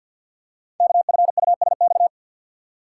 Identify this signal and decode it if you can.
Morse code, heard as a keyed tone, about 48 wpm, 690 Hz. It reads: KFUSX
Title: Code Groups - 5 Letters - 50wpm